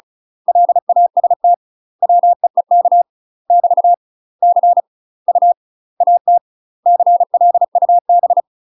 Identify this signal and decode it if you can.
Morse code, heard as a keyed tone, about 35 wpm, 700 Hz. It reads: LAST WEEK <BT> C U AT CLUB
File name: UK MASTER QSOs TRAINING / 35WPM / UK MASTER QSOs 0014